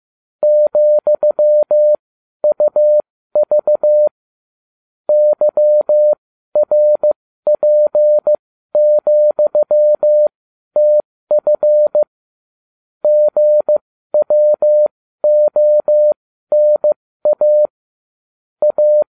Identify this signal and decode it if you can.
Morse code, heard as a keyed tone, about 15 wpm, spaced as for 12 wpm, 600 Hz. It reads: ,UV YRP,TF GWONA A